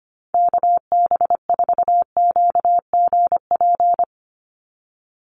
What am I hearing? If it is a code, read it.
Morse code, heard as a keyed tone, about 25 wpm, 700 Hz. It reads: KB4QGP
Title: USA Call Signs 25wpm